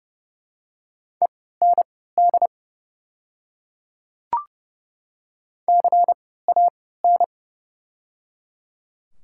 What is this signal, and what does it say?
Morse code, heard as a keyed tone, about 30 wpm, spaced as for 17 wpm, 700 Hz. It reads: END CAN
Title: 3x Character Spacing - Top 200 Words - 30wpm